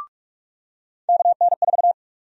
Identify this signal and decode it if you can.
Morse code, heard as a keyed tone, about 46 wpm, 700 Hz. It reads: KN4
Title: USA Call Sign Prefixes - 45wpm